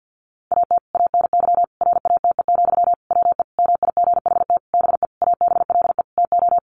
Morse code, heard as a keyed tone, about 50 wpm, 700 Hz. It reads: AT ANY RATE, WE MIGHT BE ABLE TO
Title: Sentences from Top 700 Words 50wpm